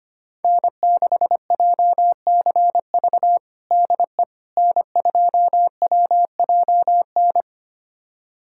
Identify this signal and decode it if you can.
Morse code, heard as a keyed tone, about 25 wpm, 700 Hz. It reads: N6JCV DE N2WJN
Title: USA Callsign DE Callsign - 5 Characters - 25wpm